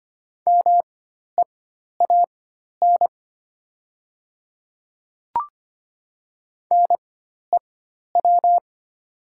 Morse code, heard as a keyed tone, about 25 wpm, spaced as for 12 wpm, 700 Hz. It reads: MEAN NEW
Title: 4x Character Spacing - Top 200 Words - 25wpm